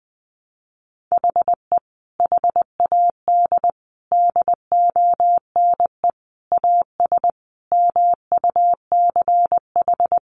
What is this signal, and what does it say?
Morse code, heard as a keyed tone, about 20 wpm, 700 Hz. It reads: HE HAD DONE AS MUCH
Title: Sentences from Top 1200 Words 20wpm